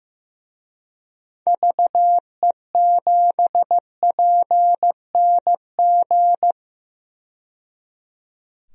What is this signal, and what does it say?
Morse code, heard as a keyed tone, about 15 wpm, 700 Hz. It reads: VE7PNG
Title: DXCC-100 Call Signs - 6 Characters - 15wpm